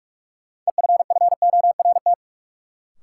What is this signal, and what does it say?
Morse code, heard as a keyed tone, about 46 wpm, 700 Hz. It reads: EFFORT
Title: Sentences from Top 1200 Words - New Word Drill - 45wpm